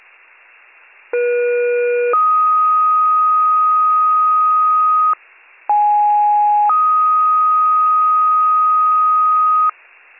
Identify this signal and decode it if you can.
Two-tone page, heard as a one-second tone, then a three-second tone, about 480 then 1250 Hz, about 830 then 1250 Hz. It is heard from 1.1 to 9.7 s.